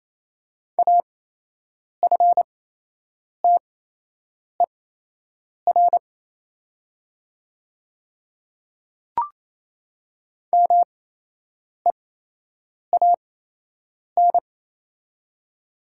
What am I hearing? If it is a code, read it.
Morse code, heard as a keyed tone, about 28 wpm, spaced as for 8 wpm, 700 Hz. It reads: AFTER MEAN
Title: 8x Character Spacing - Top 200 Words - 28wpm